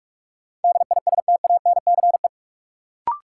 Morse code, heard as a keyed tone, about 46 wpm, 700 Hz. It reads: DISTANCE
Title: Sentences from Top 1200 Words - New Word Drill Alt - 45wpm